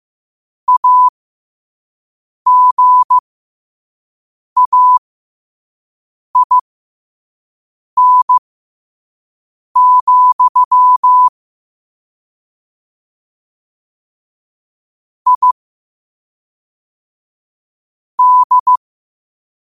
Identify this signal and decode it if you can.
Morse code, heard as a keyed tone, about 15 wpm, spaced as for 5 wpm, 1000 Hz. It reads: AGAIN, I D